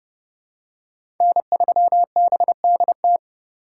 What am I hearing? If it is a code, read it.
Morse code, heard as a keyed tone, about 31 wpm, 700 Hz. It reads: N3BDT
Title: USA Call Signs - 5 Characters - 4x Character Spacing - 30wpm